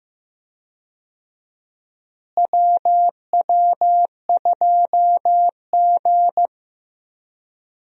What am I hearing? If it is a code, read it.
Morse code, heard as a keyed tone, about 15 wpm, 700 Hz. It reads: WW2G
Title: USA Call Signs - 4 Characters - Repeated 2x - 15wpm